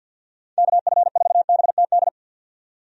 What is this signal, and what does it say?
Morse code, heard as a keyed tone, about 52 wpm, 700 Hz. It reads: KF4BTD
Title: USA Call Signs - 6 Characters - 50wpm